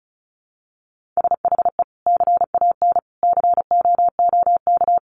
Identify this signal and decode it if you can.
Morse code, heard as a keyed tone, about 34 wpm, 700 Hz. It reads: SHE CAN COOK
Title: Sentences from Top 700 Words 35wpm